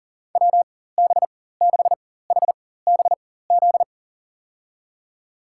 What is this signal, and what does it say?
Morse code, heard as a keyed tone, about 40 wpm, spaced as for 19 wpm, 700 Hz. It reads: WB6HBZ
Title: USA Call Signs - 6 Characters - 4x Character Spacing - 40wpm